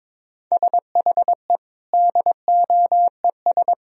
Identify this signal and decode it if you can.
Morse code, heard as a keyed tone, about 22 wpm, 700 Hz. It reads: SHE DOES
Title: Sentences from Top 1000 Words - Review Part 3 - 22wpm